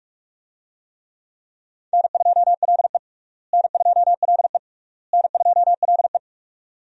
Morse code, heard as a keyed tone, about 47 wpm, 700 Hz. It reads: N2LE N2LE N2LE
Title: USA Call Signs - 4 Characters - Repeated 3x - 45wpm